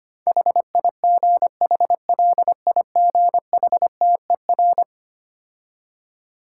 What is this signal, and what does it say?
Morse code, heard as a keyed tone, about 25 wpm, 700 Hz. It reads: HIGHLIGHTER
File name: practice / 25wpm / Difficult Words with Hints - 25wpm